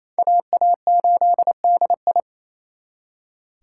Morse code, heard as a keyed tone, about 37 wpm, 700 Hz. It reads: AA8DI